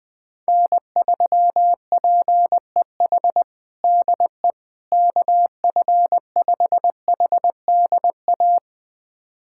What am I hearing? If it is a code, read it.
Morse code, heard as a keyed tone, about 20 wpm, 700 Hz. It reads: N3PEH DE KF5HDA
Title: USA Callsign DE Callsign 20wpm